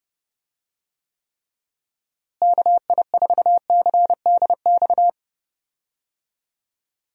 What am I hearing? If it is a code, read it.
Morse code, heard as a keyed tone, about 30 wpm, 700 Hz. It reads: KI4CDX
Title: USA Call Signs - 6 Characters - Repeated 2x - 30wpm